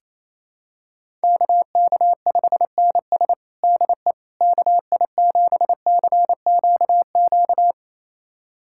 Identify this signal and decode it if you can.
Morse code, heard as a keyed tone, about 28 wpm, 700 Hz. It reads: KK5NS DE KI7CQQ
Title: USA Callsign DE Callsign 28wpm